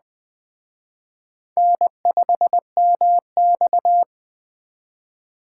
Morse code, heard as a keyed tone, about 20 wpm, 700 Hz. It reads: N5MX
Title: Speed Racing - USA 1x2 Call Signs 17wpm